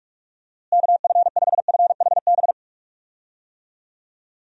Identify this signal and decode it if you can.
Morse code, heard as a keyed tone, about 43 wpm, 700 Hz. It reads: KF5FHB
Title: USA Call Signs - 6 Characters - Repeated 3x - 45wpm